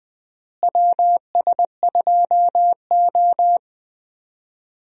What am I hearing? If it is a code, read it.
Morse code, heard as a keyed tone, about 20 wpm, 700 Hz. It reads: WS2O
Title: USA Call Signs - 4 Characters - Repeated 3x - 20wpm